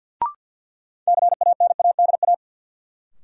Morse code, heard as a keyed tone, about 48 wpm, 700 Hz. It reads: CANADA
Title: Sets of 5 Words - New Word Drill - 50wpm